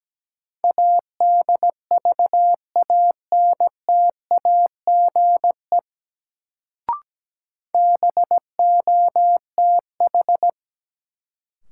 Morse code, heard as a keyed tone, about 17 wpm, 700 Hz. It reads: ADVANTAGE BOTH
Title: Sets of 4 Words - New Word Drill - 17wpm